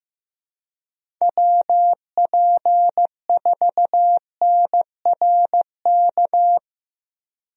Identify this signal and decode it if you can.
Morse code, heard as a keyed tone, about 15 wpm, 700 Hz. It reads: WP4NRK
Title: USA Call Signs - 6 Characters - Repeated 3x - 15wpm